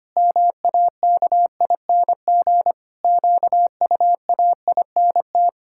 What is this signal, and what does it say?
Morse code, heard as a keyed tone, about 25 wpm, 700 Hz. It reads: MAKING QUAINT
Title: Pangram Challenge v2 - 25wpm